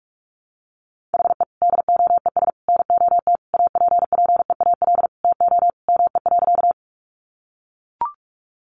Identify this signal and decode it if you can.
Morse code, heard as a keyed tone, about 44 wpm, 700 Hz. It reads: HE DOES NOT APPEAL TO ME.
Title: Sentences from Top 1300 Words 45wpm